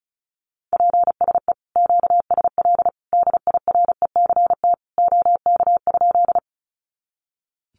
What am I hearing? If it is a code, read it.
Morse code, heard as a keyed tone, about 34 wpm, 700 Hz. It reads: PSE QSL DIRECT OK?
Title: Common QSO Phrases 35wpm